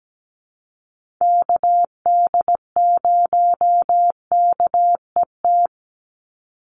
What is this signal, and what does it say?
Morse code, heard as a keyed tone, about 17 wpm, 700 Hz. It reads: KD0KET
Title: USA Call Signs 17wpm